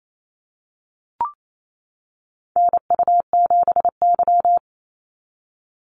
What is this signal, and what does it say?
Morse code, heard as a keyed tone, about 28 wpm, 700 Hz. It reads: NU7Y
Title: NA SOTA Activator Callsigns 28wpm